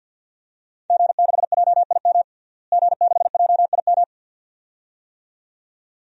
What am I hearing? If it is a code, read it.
Morse code, heard as a keyed tone, about 51 wpm, 700 Hz. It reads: G6JIM G6JIM
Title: DXCC-100 Call Signs - 5 Characters - Repeated 2x - 50wpm